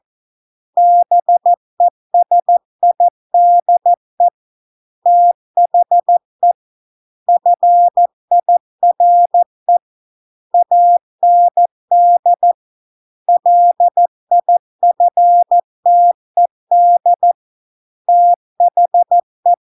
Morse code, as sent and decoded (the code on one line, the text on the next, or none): -... ... .. -.. / - .... / ..-. .. .-. / .- -. -.. / .-.. .. ..-. - . -.. / - .... .
BESIDE THE FIRE AND LIFTED THE